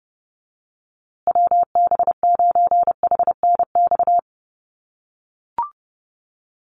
Morse code, heard as a keyed tone, about 30 wpm, 700 Hz. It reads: WB9HNX
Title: USA Call Signs Repeated 3x 30wpm